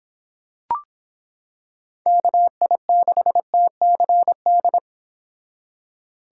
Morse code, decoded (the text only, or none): KI6TCD